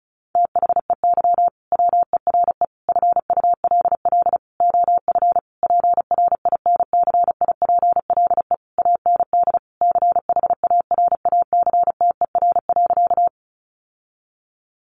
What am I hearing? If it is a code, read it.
Morse code, heard as a keyed tone, about 34 wpm, 700 Hz. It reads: THEY WERE FULL OF PRINCIPLE AND CHARACTER.
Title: Sentences from Top 1400 Words 35wpm